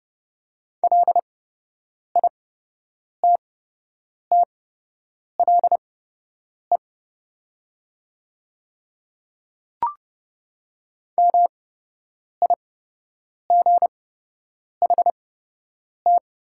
Morse code, heard as a keyed tone, about 30 wpm, spaced as for 8 wpm, 700 Hz. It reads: LITTLE MIGHT
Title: 8x Character Spacing - Top 300 Words - 30wpm